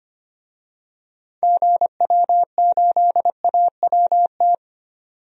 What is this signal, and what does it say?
Morse code, heard as a keyed tone, about 25 wpm, 700 Hz. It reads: GW8AWT
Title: DXCC-100 Call Signs - 6 Characters - 25wpm